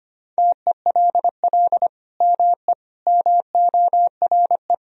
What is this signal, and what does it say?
Morse code, heard as a keyed tone, about 25 wpm, 700 Hz. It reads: TELL ME MORE
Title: Sentences from Top 1000 Words - Review Part 4 - 25wpm